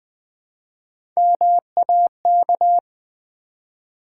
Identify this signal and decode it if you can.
Morse code, heard as a keyed tone, about 20 wpm, 700 Hz. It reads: MAK
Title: Build A Word - 1 Letter At A Time - 4-Letter Words - 20wpm